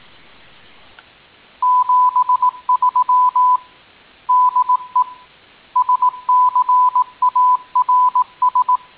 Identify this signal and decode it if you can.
Morse code, heard as a keyed tone, about 18 wpm, 1000 Hz. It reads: 73 DE SCARS